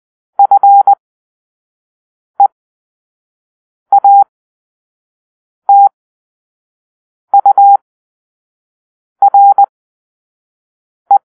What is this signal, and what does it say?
Morse code, heard as a keyed tone, about 20 wpm, spaced as for 5 wpm, 800 Hz. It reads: FEATURE
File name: arrl / Grp-4 Drill-1